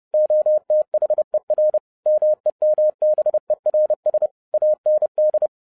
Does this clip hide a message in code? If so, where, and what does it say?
Yes: Morse code from 0.1 to 5.5 s, heard as a keyed tone, about 30 wpm, 610 Hz. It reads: OTHER MEMBERS AND